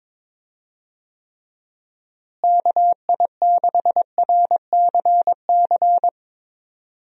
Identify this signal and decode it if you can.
Morse code, heard as a keyed tone, about 22 wpm, 700 Hz. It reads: KI6RCC
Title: USA Call Signs - 6 Characters - Repeated 2x - 22wpm